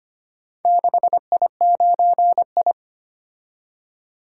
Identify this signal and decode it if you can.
Morse code, heard as a keyed tone, about 25 wpm, 700 Hz. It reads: 6I9I
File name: practice / 25wpm / Four Letter-Number Combinations - 25wpm